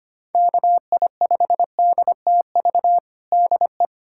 Morse code, heard as a keyed tone, about 25 wpm, 700 Hz. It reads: KI5DTV DE
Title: USA Callsign DE Callsign - 6 Characters - 25wpm